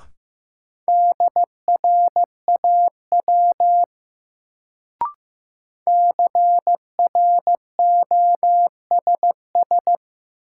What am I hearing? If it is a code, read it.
Morse code, heard as a keyed tone, about 15 wpm, 700 Hz. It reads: DRAW CROSS
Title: Sentences from Top 300 Words - New Word Drill - 15wpm